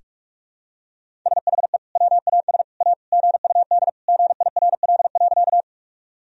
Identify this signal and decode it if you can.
Morse code, heard as a keyed tone, about 43 wpm, 700 Hz. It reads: SHE WAS A GUD GIRL.
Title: Sentences from Top 400 Words and Abbreviations 45wpm